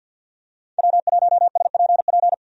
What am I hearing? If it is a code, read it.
Morse code, heard as a keyed tone, about 52 wpm, 700 Hz. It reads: W1SPP
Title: USA Call Signs - 5 Characters - 50wpm